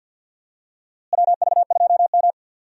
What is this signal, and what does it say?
Morse code, heard as a keyed tone, about 49 wpm, 700 Hz. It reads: WV2M
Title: USA Call Signs - 4 Characters - Repeated 2x - 50wpm